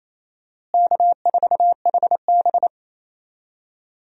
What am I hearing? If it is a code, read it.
Morse code, heard as a keyed tone, about 28 wpm, 700 Hz. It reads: K4HB
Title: USA Call Signs - 4 Characters - Repeated 3x - 28wpm